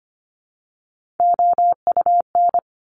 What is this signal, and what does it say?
Morse code, heard as a keyed tone, about 25 wpm, 700 Hz. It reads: OUN
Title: Top Three Letter Combinations in Words 25wpm